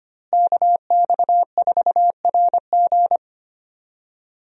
Morse code, heard as a keyed tone, about 25 wpm, 700 Hz. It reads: KX4RG
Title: USA Call Signs - 5 Characters - Repeated 2x - 25wpm